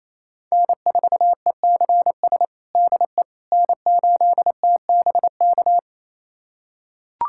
Morse code, heard as a keyed tone, about 28 wpm, 700 Hz. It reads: N4ECS DE N8TBK
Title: USA Callsign DE Callsign - 5 Characters - 28wpm